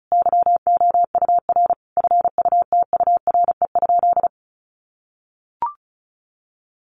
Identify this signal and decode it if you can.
Morse code, heard as a keyed tone, about 34 wpm, 700 Hz. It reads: YOUR FUTURE?
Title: Sentences from Top 700 Words 35wpm